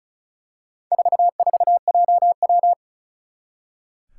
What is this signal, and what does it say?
Morse code, heard as a keyed tone, about 36 wpm, 700 Hz. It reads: 44JW
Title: Four Challenging Letter-Number Combinations 35wpm